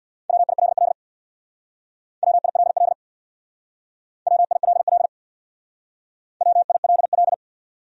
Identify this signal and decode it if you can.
Morse code, heard as a keyed tone, about 58 wpm, 690 Hz. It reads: WILL WILL WILL WILL
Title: Speed-Racing Top 500 Words 50wpm